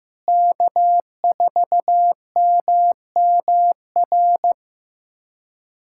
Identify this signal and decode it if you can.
Morse code, heard as a keyed tone, about 16 wpm, 700 Hz. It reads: K4MMR
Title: USA Callsign DE Callsign - 5 Characters - 15wpm